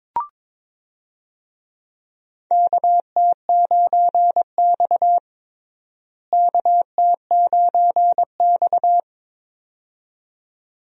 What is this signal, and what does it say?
Morse code, heard as a keyed tone, about 22 wpm, 700 Hz. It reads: KT9X KT9X
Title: USA Call Signs - 4 Characters - Repeated 2x - 22wpm